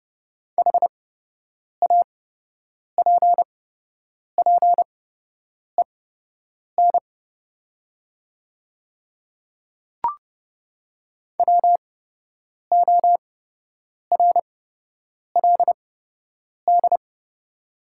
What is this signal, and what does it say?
Morse code, heard as a keyed tone, about 31 wpm, spaced as for 8 wpm, 700 Hz. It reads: HAPPEN WORLD
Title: 8x Character Spacing - Top 500 Words - 30wpm